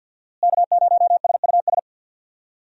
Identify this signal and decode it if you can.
Morse code, heard as a keyed tone, about 47 wpm, 700 Hz. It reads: K0SUS
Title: USA Call Signs - 5 Characters - Repeated 2x - 50wpm